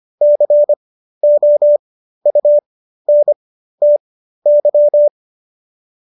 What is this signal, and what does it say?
Morse code, heard as a keyed tone, about 25 wpm, spaced as for 13 wpm, 590 Hz. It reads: COUNTY